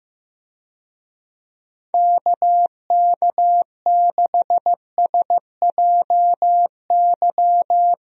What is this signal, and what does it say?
Morse code, heard as a keyed tone, about 15 wpm, 700 Hz. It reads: KK6SJY
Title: USA Call Signs - 6 Characters - 15wpm